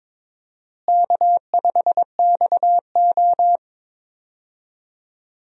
Morse code, heard as a keyed tone, about 22 wpm, 700 Hz. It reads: K5XO
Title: USA Call Signs - 4 Characters - Repeated 3x - 22wpm